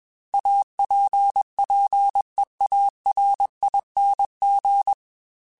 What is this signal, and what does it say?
Morse code, heard as a keyed tone, about 21 wpm, 800 Hz. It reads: APPEARING
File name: Technical Subjects / Technical Subjects-mp3s / XC @20